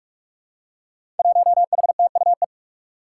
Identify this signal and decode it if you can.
Morse code, heard as a keyed tone, about 45 wpm, 700 Hz. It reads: 1HTUE
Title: Five Challenging Letter-Number Combinations 45wpm